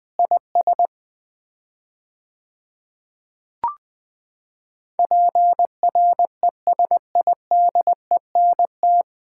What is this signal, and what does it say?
Morse code, heard as a keyed tone, about 20 wpm, 700 Hz. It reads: IS PRESIDENT
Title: 4x Word-Spacing - Sets of 5 Words - 20wpm